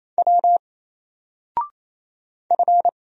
Morse code, heard as a keyed tone, about 27 wpm, 700 Hz. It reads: W F
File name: Single-Letter-Number-28wpm